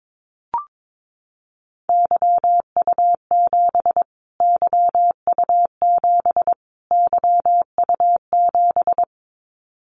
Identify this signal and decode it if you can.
Morse code, heard as a keyed tone, about 22 wpm, 700 Hz. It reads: YU7 YU7 YU7
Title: Callsign Prefixes for Top 25 DXCC - Repeated 3x - 22wpm